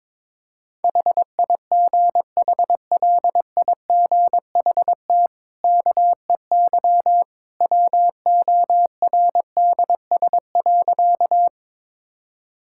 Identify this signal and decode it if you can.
Morse code, heard as a keyed tone, about 21 wpm, 690 Hz. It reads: HIGHLIGHT KEY WORDS.
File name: practice / 22wpm / Easy Sentences with Hints - 22wpm